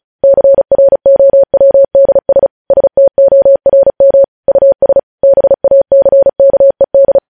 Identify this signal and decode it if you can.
Morse code, heard as a keyed tone, about 36 wpm, 550 Hz. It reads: CROWDS STORM US BACKED
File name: podcast / News headlines-1x-35x35